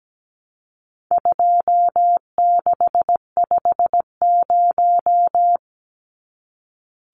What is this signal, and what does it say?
Morse code, heard as a keyed tone, about 17 wpm, 700 Hz. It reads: 2650
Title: Numbers - 4-Digit 17wpm